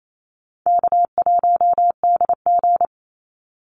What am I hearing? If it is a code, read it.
Morse code, heard as a keyed tone, about 28 wpm, 700 Hz. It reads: K1DG
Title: Easy Call Signs - 28wpm